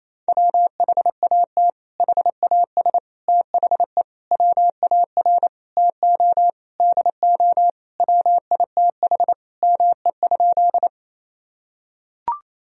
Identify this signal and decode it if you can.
Morse code, heard as a keyed tone, about 27 wpm, 700 Hz. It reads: WHAT HAS THE WAR TO DO WITH ME?
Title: Sentences from Top 1000 Words - Review Part 2 - 28wpm